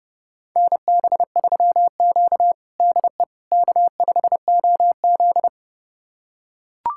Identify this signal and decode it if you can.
Morse code, heard as a keyed tone, about 30 wpm, 700 Hz. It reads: NB3Q DE K5OZ
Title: USA Callsign DE Callsign - 4 Characters - 30wpm